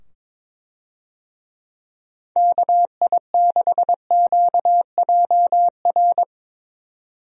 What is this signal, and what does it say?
Morse code, heard as a keyed tone, about 22 wpm, 700 Hz. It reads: KI6QJR
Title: USA Call Signs - 6 Characters - 22wpm